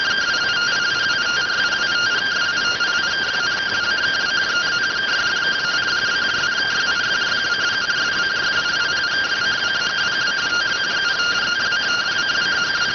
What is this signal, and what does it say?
Two-tone teleprinter signal, about 170 Hz shift.